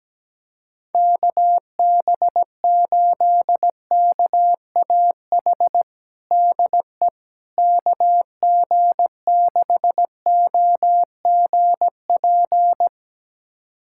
Morse code, heard as a keyed tone, about 17 wpm, 700 Hz. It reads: KB8KAH DE KG6OGP
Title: USA Callsign DE Callsign - 6 Characters - 17wpm